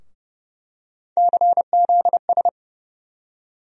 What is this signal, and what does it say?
Morse code, heard as a keyed tone, about 30 wpm, 700 Hz. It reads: CZS
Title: USA Call Sign Postfixes - 3 Letters 30wpm